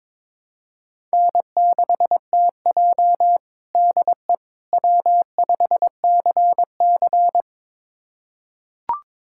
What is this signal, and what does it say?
Morse code, heard as a keyed tone, about 22 wpm, 700 Hz. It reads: N6TJ DE W5CC
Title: USA Callsign DE Callsign - 4 Characters - 22wpm